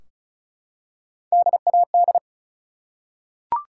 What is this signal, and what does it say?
Morse code, heard as a keyed tone, about 36 wpm, 700 Hz. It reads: DAD
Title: Three Letter Words - 35wpm